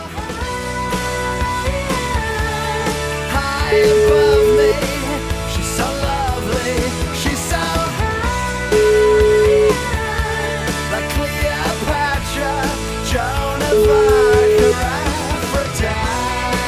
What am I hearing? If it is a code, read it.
Ringback tone (Europe, 425 Hz), with noise.